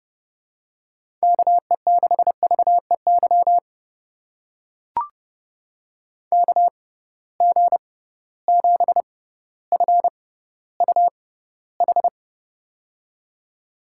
Morse code, heard as a keyed tone, about 30 wpm, spaced as for 15 wpm, 700 Hz. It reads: KE6VEY KG7FUH
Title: USA Call Signs - 6 Characters - 6x Character Spacing - 30wpm